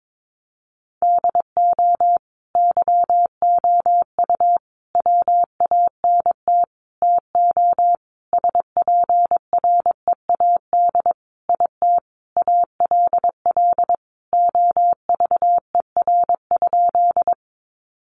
Morse code, heard as a keyed tone, about 22 wpm, 700 Hz. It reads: DO YOU WANT TO SPREAD IT ALL OVER?
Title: Sentences from Top 1200 Words 22wpm